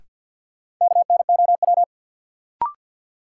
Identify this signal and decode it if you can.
Morse code, heard as a keyed tone, about 52 wpm, 700 Hz. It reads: KNOW
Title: Sets of 2 Words - New Word Drill - 50wpm